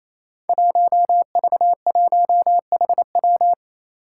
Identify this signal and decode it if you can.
Morse code, heard as a keyed tone, about 28 wpm, 700 Hz. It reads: 1V1HW
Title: Five Challenging Letter-Number Combinations 28wpm